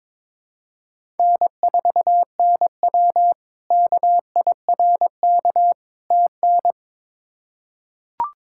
Morse code, heard as a keyed tone, about 22 wpm, 700 Hz. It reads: N4NW KIRK TN
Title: K1USN SST QSO Practice - 22wpm